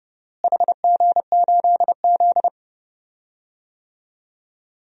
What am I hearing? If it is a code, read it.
Morse code, heard as a keyed tone, about 32 wpm, 700 Hz. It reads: HG8Z
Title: DXCC-100 Call Signs - 4 Characters - Repeated 3x - 30wpm